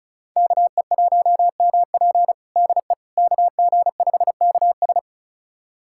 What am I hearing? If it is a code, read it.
Morse code, heard as a keyed tone, about 36 wpm, 700 Hz. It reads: KE1MP DE KG5KS
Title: USA Callsign DE Callsign - 5 Characters - 35wpm